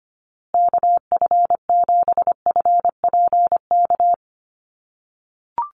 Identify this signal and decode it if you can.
Morse code, heard as a keyed tone, about 25 wpm, 700 Hz. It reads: KF7FPK
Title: USA Call Signs Repeated 2x 25wpm